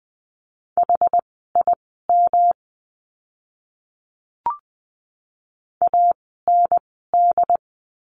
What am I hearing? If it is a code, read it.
Morse code, heard as a keyed tone, about 20 wpm, spaced as for 15 wpm, 700 Hz. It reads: HIM AND
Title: 244 Lesson 39f - Letter z - Words 20wpm